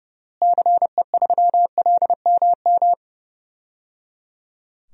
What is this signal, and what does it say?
Morse code, heard as a keyed tone, about 30 wpm, 700 Hz. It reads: CE3LMM
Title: DXCC-100 Call Signs - 6 Characters - Repeated 3x - 30wpm